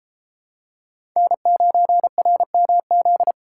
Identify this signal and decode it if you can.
Morse code, heard as a keyed tone, about 29 wpm, 700 Hz. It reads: N9RMZ